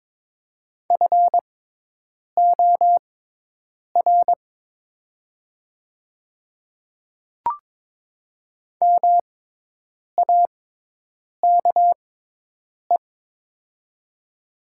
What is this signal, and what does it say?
Morse code, heard as a keyed tone, about 23 wpm, spaced as for 8 wpm, 700 Hz. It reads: FOR MAKE
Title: 6x Character Spacing - Top 100 Words - 22wpm